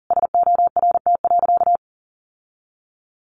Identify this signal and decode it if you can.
Morse code, heard as a keyed tone, about 40 wpm, 700 Hz. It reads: SORT.
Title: Sentences from Top 1300 Words 40wpm